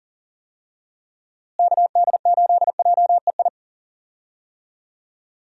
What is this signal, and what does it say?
Morse code, heard as a keyed tone, about 39 wpm, 700 Hz. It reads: KD8JEI
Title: USA Call Signs - 6 Characters - Repeated 3x - 40wpm